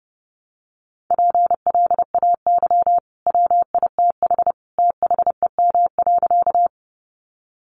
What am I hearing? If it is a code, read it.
Morse code, heard as a keyed tone, about 30 wpm, 700 Hz. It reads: PLAY WITH THEM.
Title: Sentences from Top 200 Words 30wpm